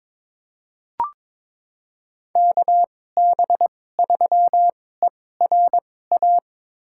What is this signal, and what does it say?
Morse code, heard as a keyed tone, about 22 wpm, spaced as for 16 wpm, 700 Hz. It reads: KB3ERA
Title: USA Call Signs - 6 Characters - 2x Character Spacing - 22wpm